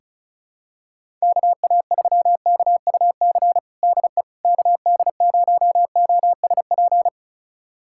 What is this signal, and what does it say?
Morse code, heard as a keyed tone, about 36 wpm, 700 Hz. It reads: KA3KUC DE KD0OSP
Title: USA Callsign DE Callsign 35wpm